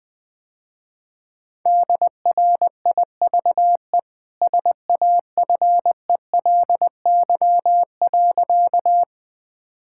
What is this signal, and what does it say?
Morse code, heard as a keyed tone, about 21 wpm, 700 Hz. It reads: DRIVE SAFELY.
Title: Easy Sentences with Hints - 20wpm